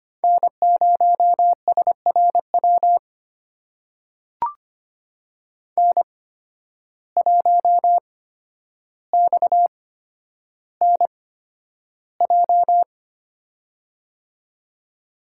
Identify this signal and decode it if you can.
Morse code, heard as a keyed tone, about 25 wpm, spaced as for 11 wpm, 700 Hz. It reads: N0SRW N1XNJ
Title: USA Call Signs - 5 Characters - 8x Character Spacing - 25wpm